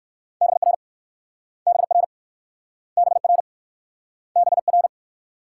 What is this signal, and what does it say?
Morse code, heard as a keyed tone, about 60 wpm, 700 Hz. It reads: BR BR BR BR